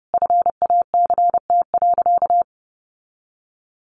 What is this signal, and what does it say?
Morse code, heard as a keyed tone, about 28 wpm, 700 Hz. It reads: FACT.
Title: Sentences from Top 1300 Words 30wpm